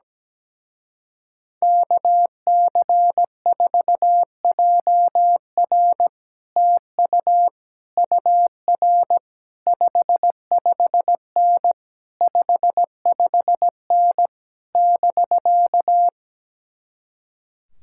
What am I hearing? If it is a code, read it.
Morse code, heard as a keyed tone, about 17 wpm, 700 Hz. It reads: KC4JR TU UR 55N 55N <BK>
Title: Speed-Racing Common QSO Phrases Encore - 17wpm